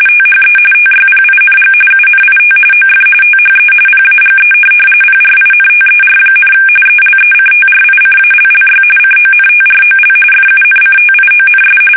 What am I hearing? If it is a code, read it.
RTTY, about 45.45 baud. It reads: THE TYPING UNIT OF THE PAGE PRINTER, ELECTRICALLY CONTROL FUNCTIONS
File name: fsk850 / teletype-corp